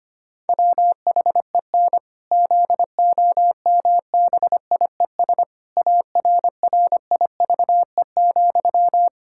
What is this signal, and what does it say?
Morse code, heard as a keyed tone, about 25 wpm, 700 Hz. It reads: WHEN ZOMBIES ARRIVE,
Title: Pangram Challenge v2 - 25wpm